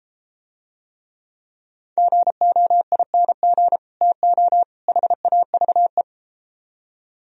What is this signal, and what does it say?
Morse code, heard as a keyed tone, about 32 wpm, 700 Hz. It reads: GOING TO HAVE